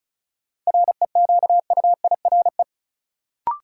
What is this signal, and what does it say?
Morse code, heard as a keyed tone, about 34 wpm, 700 Hz. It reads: REQUIRE
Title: Sentences from Top 900 Words - New Word Drill - 35wpm